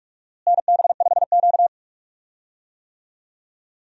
Morse code, heard as a keyed tone, about 46 wpm, 700 Hz. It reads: NB5Q
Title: USA Call Signs - 4 Characters - Repeated 3x - 45wpm